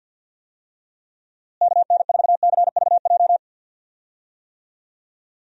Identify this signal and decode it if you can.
Morse code, heard as a keyed tone, about 50 wpm, 700 Hz. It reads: KN4CVJ
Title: USA Call Signs - 6 Characters - Repeated 2x - 50wpm